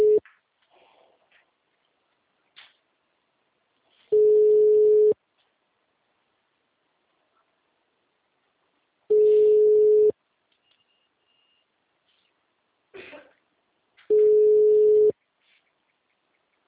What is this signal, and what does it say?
Ringback tone (Europe, 425 Hz).
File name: out- -2016- - - .30562